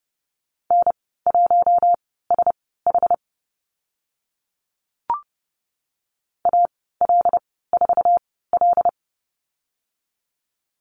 Morse code, heard as a keyed tone, about 30 wpm, spaced as for 17 wpm, 700 Hz. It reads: N1SH AL4L
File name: Lesson 14g - Number 2 - Call Signs 30wpm